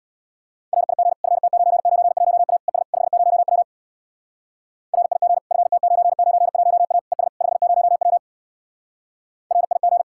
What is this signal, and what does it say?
Morse code, heard as a keyed tone, about 64 wpm, 700 Hz. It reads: RIG FT991A ES 50W RIG FT991A ES 50W RIG